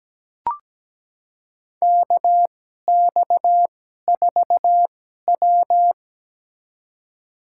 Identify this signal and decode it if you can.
Morse code, heard as a keyed tone, about 17 wpm, spaced as for 12 wpm, 700 Hz. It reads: KX4W
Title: USA Call Signs - 4 Characters - 2x Character Spacing - 17wpm